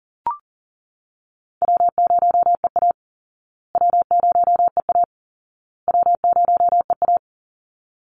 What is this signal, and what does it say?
Morse code, heard as a keyed tone, about 40 wpm, 700 Hz. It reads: W0EA W0EA W0EA
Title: Easy Call Signs Repeated 3x - 40wpm